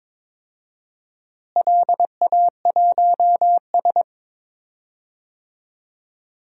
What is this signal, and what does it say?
Morse code, heard as a keyed tone, about 22 wpm, 700 Hz. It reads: LA1S
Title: DXCC-100 Call Signs - 4 Characters - Repeated 2x - 22wpm